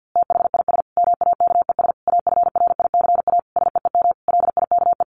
Morse code, heard as a keyed tone, about 50 wpm, 700 Hz. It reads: THIS MAKES AFRICA SEEM LIKE
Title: A Sound Of Thunder 50wpm